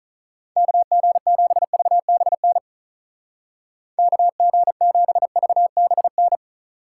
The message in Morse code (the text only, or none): KG7VBN KG7VBN